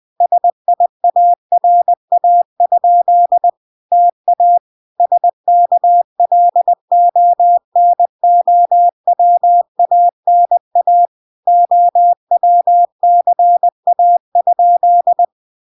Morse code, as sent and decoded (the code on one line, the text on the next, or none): ... .. .- .-. .- ..--.. / - .- / ... -.- .-.. --- -. --- .-- .- -. .- / --- .-- -.-. .- ..--..
SIARA? TA SKLONOWANA OWCA?